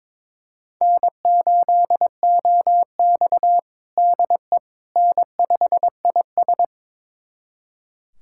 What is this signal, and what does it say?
Morse code, heard as a keyed tone, about 22 wpm, 700 Hz. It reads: N8OX DE N5IS
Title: USA Callsign DE Callsign - 4 Characters - 22wpm